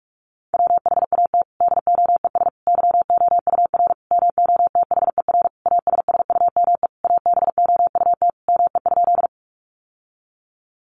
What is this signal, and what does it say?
Morse code, heard as a keyed tone, about 44 wpm, 700 Hz. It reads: WHAT DOES YOUR MOTHER ASSUME ABOUT ME?
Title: Sentences from Top 1500 Words 45wpm